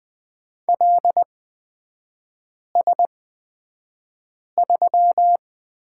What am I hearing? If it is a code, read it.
Morse code, heard as a keyed tone, about 20 wpm, spaced as for 5 wpm, 700 Hz. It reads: LS3
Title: 243 Lesson 39e - Letter z - Three Characters 20wpm